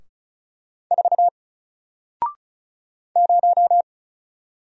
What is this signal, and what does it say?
Morse code, heard as a keyed tone, about 36 wpm, 700 Hz. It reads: 4 0
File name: Single-Letter-Number-35wpm